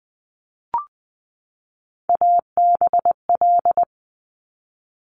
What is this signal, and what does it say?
Morse code, heard as a keyed tone, about 20 wpm, 700 Hz. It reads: ABL
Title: Top Three Letter Combinations in Words 20wpm